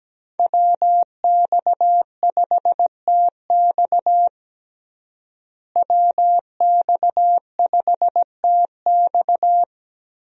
Morse code, heard as a keyed tone, about 17 wpm, 700 Hz. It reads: WX5TX WX5TX
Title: USA Call Signs - 5 Characters - Repeated 3x - 17wpm